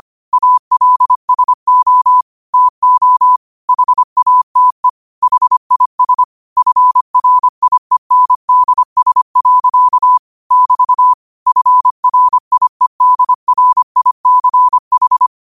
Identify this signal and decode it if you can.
Morse code, heard as a keyed tone, about 25 wpm, 1000 Hz. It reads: ALSO TO HATE HIS FRIENDS. <BT> FRIEDRICH
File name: qotdfiles / 25wpm